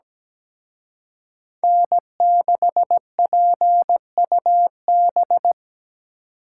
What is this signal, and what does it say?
Morse code, heard as a keyed tone, about 20 wpm, 700 Hz. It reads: N6PUB
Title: Speed Racing - USA 1x3 Call Signs 17wpm